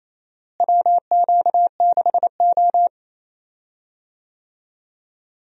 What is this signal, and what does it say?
Morse code, heard as a keyed tone, about 28 wpm, 700 Hz. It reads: WQ6O
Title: USA Call Signs - 4 Characters - Repeated 3x - 28wpm